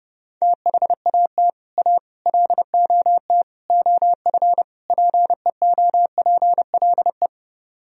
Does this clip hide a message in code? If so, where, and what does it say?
Yes: Morse code from 0.4 to 7.3 s, heard as a keyed tone, about 30 wpm, 700 Hz. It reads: THAT A LOT OF PEOPLE